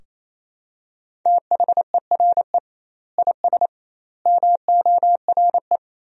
Morse code, heard as a keyed tone, about 27 wpm, 700 Hz. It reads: THERE IS MORE
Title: 2x Word-Spacing - Sentences from Top 200 Words - 28wpm